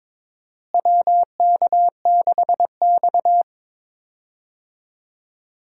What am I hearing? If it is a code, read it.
Morse code, heard as a keyed tone, about 22 wpm, 700 Hz. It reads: WK6X
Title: USA Call Signs - 4 Characters - Repeated 2x - 22wpm